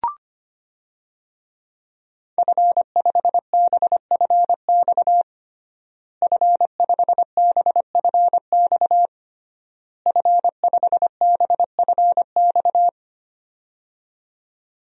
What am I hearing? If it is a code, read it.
Morse code, heard as a keyed tone, about 25 wpm, 700 Hz. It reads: F5BFX F5BFX F5BFX
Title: DXCC-100 Call Signs - 5 Characters - Repeated 3x - 25wpm